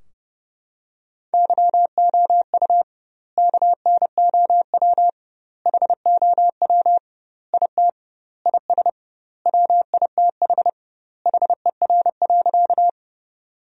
Morse code, heard as a keyed tone, about 30 wpm, 700 Hz. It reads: YOU KNOW HOW IT IS WITH HER.
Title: 2x Word-Spacing - Sentences from Top 100 Words - 30wpm